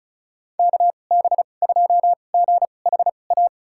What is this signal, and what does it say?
Morse code, heard as a keyed tone, about 36 wpm, spaced as for 25 wpm, 700 Hz. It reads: KB2GHA
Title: USA Call Signs - 6 Characters - 2x Character Spacing - 35wpm